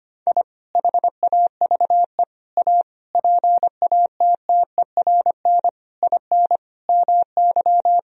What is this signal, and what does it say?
Morse code, heard as a keyed tone, about 25 wpm, 700 Hz. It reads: I HAVE A PATTERN IN MY
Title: Sentences from Top 1000 Words - Review Part 2 - 25wpm